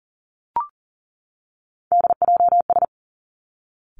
Morse code, heard as a keyed tone, about 42 wpm, 700 Hz. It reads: DJS
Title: Three Challenging Letter-Number Combinations 40wpm